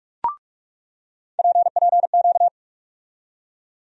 Morse code, heard as a keyed tone, about 43 wpm, 700 Hz. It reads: PPQ
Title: USA Call Sign Postfixes - 3 Letters 45wpm